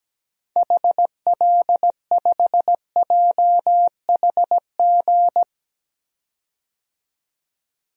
Morse code, heard as a keyed tone, about 17 wpm, 700 Hz. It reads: HL5JHG
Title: DXCC-100 Call Signs - 6 Characters - 17wpm